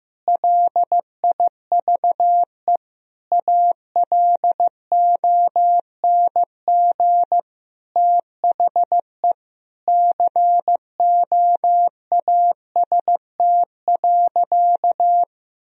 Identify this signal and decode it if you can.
Morse code, heard as a keyed tone, about 15 wpm, 700 Hz. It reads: LIVE ALONG THE COAST.
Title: Sentences from Top 1000 Words - Review Part 4 - 15wpm